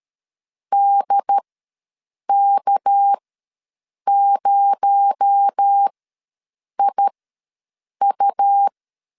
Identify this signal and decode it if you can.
Morse code, heard as a keyed tone, about 13 wpm, spaced as for 7 wpm, 800 Hz. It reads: DK0IU
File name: LEZIONI / 6 PKX0 / 6 PKX0 ws 12wpm 3448